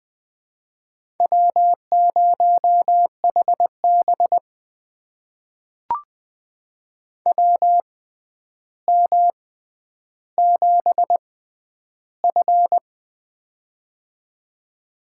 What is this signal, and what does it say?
Morse code, heard as a keyed tone, about 20 wpm, spaced as for 10 wpm, 700 Hz. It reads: W0HB WM7F
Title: USA Call Signs - 4 Characters - 6x Character Spacing - 20wpm